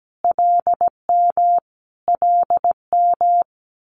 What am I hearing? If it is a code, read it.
Morse code, heard as a keyed tone, about 17 wpm, 690 Hz. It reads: LM LM
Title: Multi-letter Phonemes Repeated 3x 17wpm